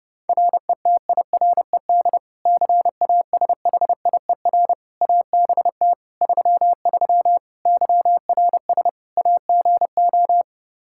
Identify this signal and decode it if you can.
Morse code, heard as a keyed tone, about 31 wpm, 700 Hz. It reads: RETIRED CASHIER ABT 33 YRS AGO
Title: Practice QSO - Exchange 3 - More Info - 30wpm